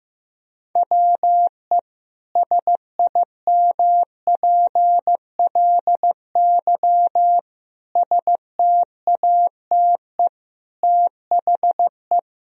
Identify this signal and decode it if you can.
Morse code, heard as a keyed tone, about 16 wpm, 700 Hz. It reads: WE SIMPLY STATE THE
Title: Sentences from Top 1000 Words - Review Part 4 - 15wpm